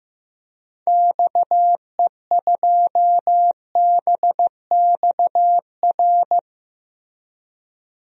Morse code, heard as a keyed tone, about 15 wpm, 700 Hz. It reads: XE2BXR
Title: DXCC-100 Call Signs - 6 Characters - Repeated 3x - 15wpm